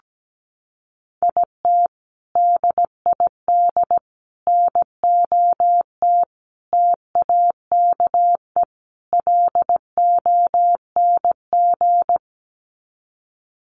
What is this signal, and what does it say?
Morse code, heard as a keyed tone, about 17 wpm, 700 Hz. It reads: IT DID NOT TAKE LONG
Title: Sets of 5 Words Encore2 17wpm